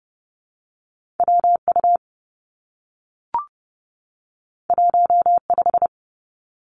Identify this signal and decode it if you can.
Morse code, heard as a keyed tone, about 31 wpm, 700 Hz. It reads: W U 1 5
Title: Two Challenging Letter-Number Combinations 30wpm